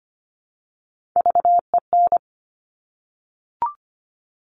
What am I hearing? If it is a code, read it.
Morse code, heard as a keyed tone, about 25 wpm, 700 Hz. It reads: VEN
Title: Top Three Letter Combinations in Words 25wpm